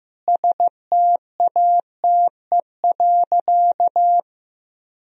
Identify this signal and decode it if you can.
Morse code, heard as a keyed tone, about 15 wpm, 700 Hz. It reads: STATE.
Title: Sentences from Top 1000 Words - Review Part 2 - 15wpm